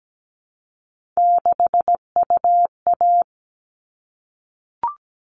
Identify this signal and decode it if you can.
Morse code, heard as a keyed tone, about 17 wpm, 700 Hz. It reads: 6UA
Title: All Cut Numbers - 3-Digit 17wpm